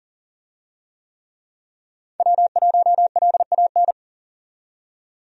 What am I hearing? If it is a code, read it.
Morse code, heard as a keyed tone, about 40 wpm, 700 Hz. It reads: W1LAN
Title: USA Call Signs - 5 Characters - 40wpm